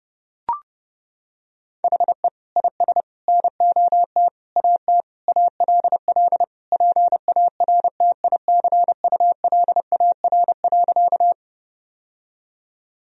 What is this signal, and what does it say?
Morse code, heard as a keyed tone, about 30 wpm, 700 Hz. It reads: HE IS NOT AT ALL PARTICULAR.
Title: Sentences from Top 1000 Words - Review Part 4 - 30wpm